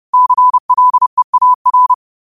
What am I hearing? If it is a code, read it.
Morse code, heard as a keyed tone, about 30 wpm, 1000 Hz. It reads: CLEAR
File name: qotdfiles / 30wpm